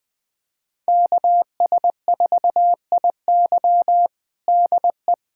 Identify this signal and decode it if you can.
Morse code, heard as a keyed tone, about 20 wpm, 700 Hz. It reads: KS4IY DE
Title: USA Callsign DE Callsign - 5 Characters - 20wpm